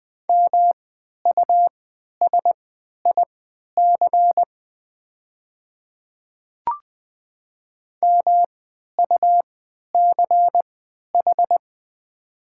Morse code, heard as a keyed tone, about 20 wpm, spaced as for 11 wpm, 700 Hz. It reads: MUSIC MUCH
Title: 3x Character Spacing - Top 300 Words - 20wpm